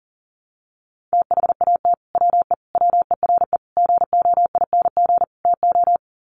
Morse code, heard as a keyed tone, about 40 wpm, 700 Hz. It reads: THAT WE WERE GOING TO
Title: Sets of 5 Words Encore2 40wpm